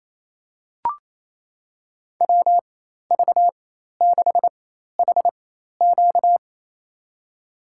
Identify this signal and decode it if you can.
Morse code, heard as a keyed tone, about 28 wpm, spaced as for 13 wpm, 700 Hz. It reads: WV6HQ
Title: USA Call Signs - 5 Characters - 4x Character Spacing - 28wpm